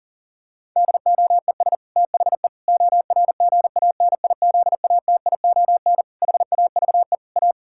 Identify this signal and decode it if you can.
Morse code, heard as a keyed tone, about 37 wpm, 700 Hz. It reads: DOES THE ORGANIZATION HAVE A
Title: Sentences from Top 1000 Words - Review Part 6 - 40wpm